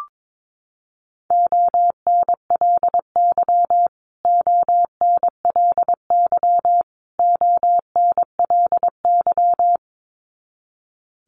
Morse code, heard as a keyed tone, about 22 wpm, 700 Hz. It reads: ONLY ONLY ONLY
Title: Top 300 Words Repeated 22wpm